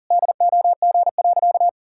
Morse code, heard as a keyed tone, about 37 wpm, 700 Hz. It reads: DOG.
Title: Pangram Challenge v2 - 40wpm